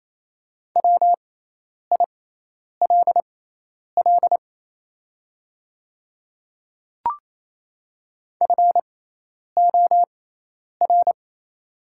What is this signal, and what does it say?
Morse code, heard as a keyed tone, about 27 wpm, spaced as for 10 wpm, 700 Hz. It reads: WILL FOR
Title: 6x Character Spacing - Top 100 Words - 28wpm